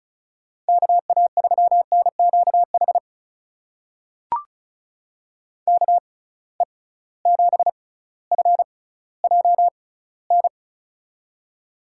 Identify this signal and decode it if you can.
Morse code, heard as a keyed tone, about 34 wpm, spaced as for 18 wpm, 700 Hz. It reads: KA3NQH KE7FJN
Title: USA Call Signs - 6 Characters - 6x Character Spacing - 35wpm